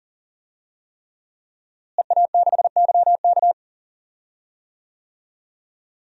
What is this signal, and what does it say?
Morse code, heard as a keyed tone, about 37 wpm, 700 Hz. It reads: EA6YK
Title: DXCC-100 Call Signs - 5 Characters - Repeated 3x - 40wpm